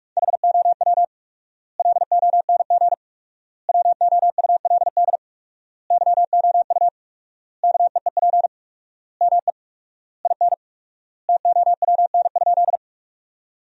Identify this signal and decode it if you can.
Morse code, heard as a keyed tone, about 46 wpm, 700 Hz. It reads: HOW LONG WOULD YOU KEEP ME IN TOWN?
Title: 4x Word-Spacing - Sentences from Top 500 Words - 45wpm